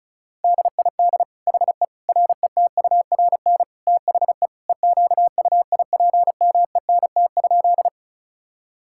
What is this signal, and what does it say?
Morse code, heard as a keyed tone, about 34 wpm, 700 Hz. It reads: DID HE RETURN THE EQUIPMENT?
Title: Sentences from Top 1000 Words - Review Part 5 - 35wpm